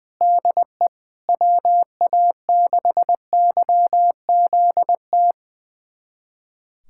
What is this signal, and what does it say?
Morse code, heard as a keyed tone, about 20 wpm, 700 Hz. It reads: DE WA6YZT
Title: USA Callsign DE Callsign - 6 Characters - 20wpm